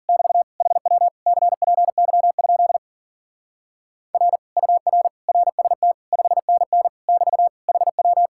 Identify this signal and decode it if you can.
Morse code, heard as a keyed tone, about 47 wpm, 700 Hz. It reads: <BT> HW CPY? R UR RST 5NN <BT> HW
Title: Speed-Racing Common QSO Phrases Encore - 40wpm